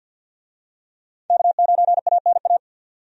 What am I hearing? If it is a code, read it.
Morse code, heard as a keyed tone, about 52 wpm, 700 Hz. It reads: K9ANA
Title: USA Call Signs - 5 Characters - 50wpm